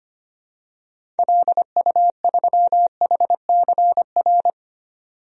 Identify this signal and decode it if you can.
Morse code, heard as a keyed tone, about 25 wpm, 700 Hz. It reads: LU3HCR
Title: DXCC-100 Call Signs - 6 Characters - 25wpm